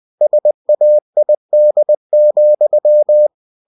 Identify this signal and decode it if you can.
Morse code, heard as a keyed tone, about 20 wpm, 600 Hz. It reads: SAID,
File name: cw-books / winnie-the-pooh / winnie-20wpm-page-0102